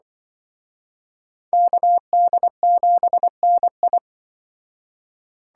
Morse code, heard as a keyed tone, about 25 wpm, 700 Hz. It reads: KD7NI
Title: Speed Racing - USA 2x2 Call Signs 20wpm